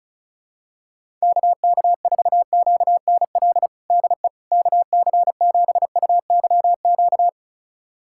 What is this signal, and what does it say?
Morse code, heard as a keyed tone, about 36 wpm, 700 Hz. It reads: KK4QNL DE KC7UYQ
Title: USA Callsign DE Callsign - 6 Characters - 35wpm